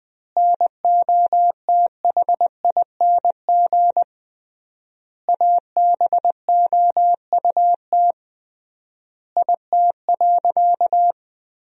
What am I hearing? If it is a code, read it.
Morse code, heard as a keyed tone, about 20 wpm, 700 Hz. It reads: NOTHING ABOUT IT.
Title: 3x Word-Spacing - Sentences from Top 500 Words - 20wpm